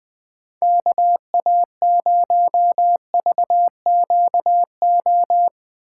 Morse code, heard as a keyed tone, about 20 wpm, 700 Hz. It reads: KA0VQO
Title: USA Call Signs - 6 Characters - 3x Character Spacing - 20wpm